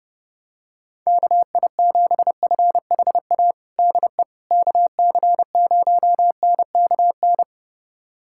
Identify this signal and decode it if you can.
Morse code, heard as a keyed tone, about 30 wpm, 700 Hz. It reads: KI7FHA DE KC0NKN
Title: USA Callsign DE Callsign 30wpm